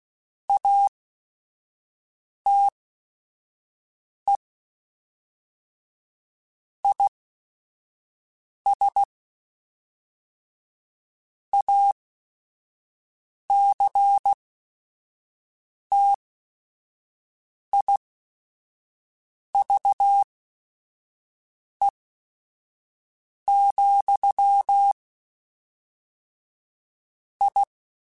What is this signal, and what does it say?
Morse code, heard as a keyed tone, about 16 wpm, spaced as for 5 wpm, 800 Hz. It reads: ATE IS ACTIVE, I
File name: Technical Subjects / Technical Subjects-mp3s / AND Gate @5